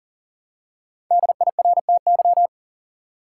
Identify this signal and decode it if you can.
Morse code, heard as a keyed tone, about 40 wpm, 700 Hz. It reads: DIRTY
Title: Five Letter Words - 40wpm